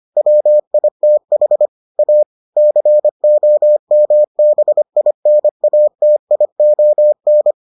Morse code, heard as a keyed tone, about 25 wpm, 600 Hz. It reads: WITH A COMBINATION